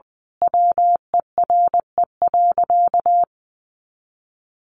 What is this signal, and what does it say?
Morse code, heard as a keyed tone, about 20 wpm, 700 Hz. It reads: WERE.
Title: Sentences from Top 1000 Words 20wpm